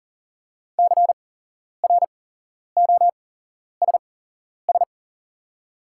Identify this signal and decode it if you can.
Morse code, heard as a keyed tone, about 40 wpm, spaced as for 11 wpm, 690 Hz. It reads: CROSS
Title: 8x Character Spacing - Top 300 Words - 40wpm